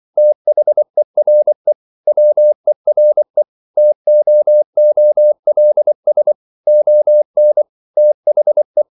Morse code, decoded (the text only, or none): THERE WERE TOOLS ON THE